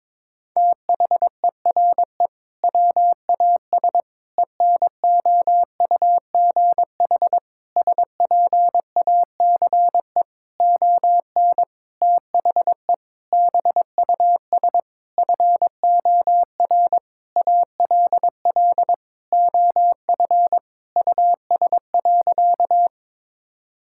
Morse code, heard as a keyed tone, about 22 wpm, 700 Hz. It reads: THERE WAS ENOUGH SPACE ON THE BUS FOR ALL OF US.
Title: Sentences from Top 1000 Words - Review Part 6 - 22wpm